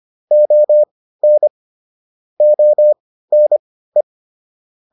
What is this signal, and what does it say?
Morse code, heard as a keyed tone, about 25 wpm, spaced as for 15 wpm, 600 Hz. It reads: ON ONE